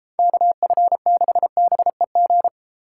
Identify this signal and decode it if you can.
Morse code, heard as a keyed tone, about 32 wpm, 700 Hz. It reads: KF6BEG